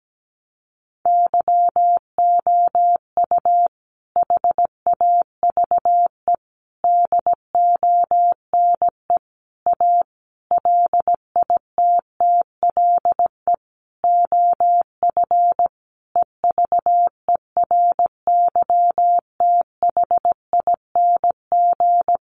Morse code, heard as a keyed tone, about 17 wpm, 700 Hz. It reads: YOU HAVE DONE A LITTLE OF EVERYTHING
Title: Sentences from Top 700 Words 17wpm